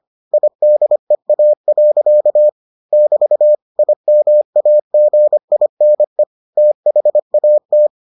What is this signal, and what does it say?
Morse code, heard as a keyed tone, about 25 wpm, 600 Hz. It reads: IDEA. <BT> IMAGINE THAT